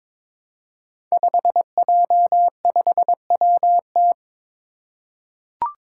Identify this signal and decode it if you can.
Morse code, heard as a keyed tone, about 22 wpm, 700 Hz. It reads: 5J5WT
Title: Five Challenging Letter-Number Combinations 22wpm